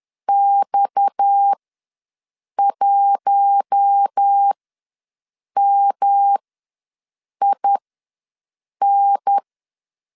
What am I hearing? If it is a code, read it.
Morse code, heard as a keyed tone, about 10 wpm, spaced as for 6 wpm, 800 Hz. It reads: X1MIN
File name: LEZIONI / 6 PKX0 / 6 PKX ws 10wpm 3706